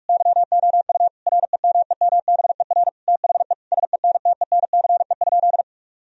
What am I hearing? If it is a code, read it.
Morse code, heard as a keyed tone, about 46 wpm, 700 Hz. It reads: YOU REMEMBER THE SENTENCE?
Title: Sentences from Top 1000 Words - Review Part 1 - 45wpm